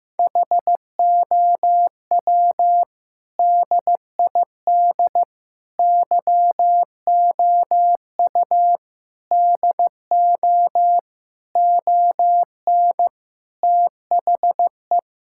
Morse code, heard as a keyed tone, about 15 wpm, 700 Hz. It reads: HOW DID YOU DO ON THE
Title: Sentences from Top 1000 Words - Review Part 5 - 15wpm